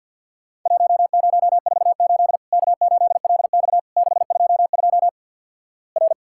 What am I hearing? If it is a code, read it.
Morse code, heard as a keyed tone, about 50 wpm, 700 Hz. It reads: 1048 K8LX 622 R
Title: Worked All Europe DX Contest - QTC Exchange - 50wpm